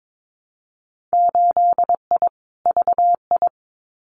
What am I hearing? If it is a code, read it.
Morse code, heard as a keyed tone, about 22 wpm, 700 Hz. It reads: 8I VI
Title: Winter Field Day Exchanges Encore 22wpm